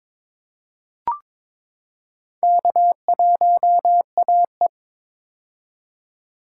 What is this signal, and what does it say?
Morse code, heard as a keyed tone, about 22 wpm, 700 Hz. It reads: K1AE
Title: K1USN SST QSO Practice - 22wpm